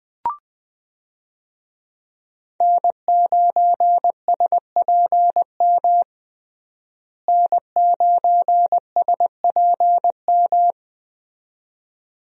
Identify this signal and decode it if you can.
Morse code, heard as a keyed tone, about 20 wpm, 700 Hz. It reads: N9SPM N9SPM
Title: USA Call Signs - 5 Characters - Repeated 2x - 20wpm